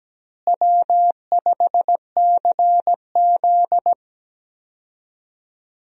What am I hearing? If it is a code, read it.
Morse code, heard as a keyed tone, about 17 wpm, 700 Hz. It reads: W5CZ
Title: USA Call Signs - 4 Characters - Repeated 2x - 17wpm